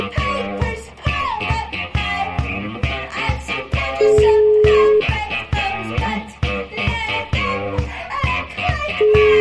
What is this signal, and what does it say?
Ringback tone (Europe, 425 Hz), with noise.